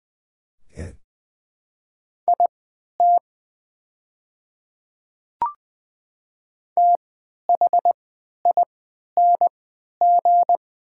Morse code, heard as a keyed tone, about 20 wpm, spaced as for 11 wpm, 700 Hz. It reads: IT THING
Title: 3x Character Spacing - Top 100 Words - 20wpm